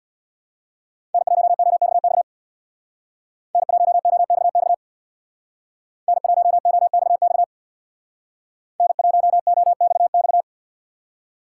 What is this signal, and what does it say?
Morse code, heard as a keyed tone, about 64 wpm, 700 Hz. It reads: N1OXX N1OXX N1OXX N1OXX